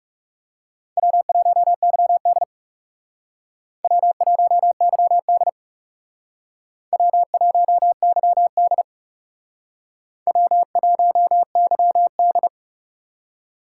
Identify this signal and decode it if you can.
Morse code, heard as a keyed tone, about 37 wpm, 700 Hz. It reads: W1YD W1YD W1YD W1YD
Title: Speed Racing - USA Call Signs 30wpm